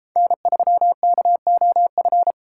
Morse code, heard as a keyed tone, about 33 wpm, 700 Hz. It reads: N3KOF